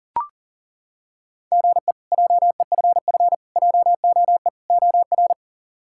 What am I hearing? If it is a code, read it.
Morse code, heard as a keyed tone, about 40 wpm, 700 Hz. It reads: GE JEFF JOE OR
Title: K1USN SST QSO Practice - 40wpm